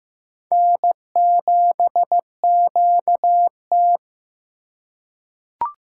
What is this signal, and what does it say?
Morse code, heard as a keyed tone, about 16 wpm, 700 Hz. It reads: N7QT
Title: K1USN SST QSO Practice - 15wpm